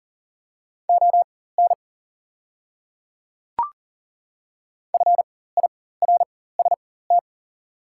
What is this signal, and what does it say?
Morse code, heard as a keyed tone, about 40 wpm, spaced as for 19 wpm, 690 Hz. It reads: ON FIRST
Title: 4x Character Spacing - Top 100 Words - 40wpm